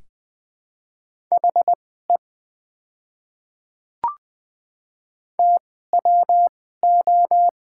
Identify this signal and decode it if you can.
Morse code, heard as a keyed tone, about 20 wpm, spaced as for 15 wpm, 690 Hz. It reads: HE TWO
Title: 2x Character Spacing - Top 100 Words - 20wpm